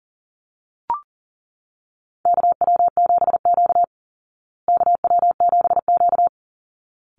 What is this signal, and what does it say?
Morse code, heard as a keyed tone, about 40 wpm, 700 Hz. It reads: KW7Q KW7Q
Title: Call Signs Repeated 3x - 40wpm